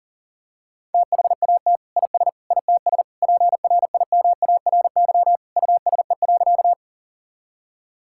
Morse code, heard as a keyed tone, about 39 wpm, 700 Hz. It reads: THAT IS ITS PRIMARY USE.
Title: Sentences from Top 1000 Words - Review Part 5 - 40wpm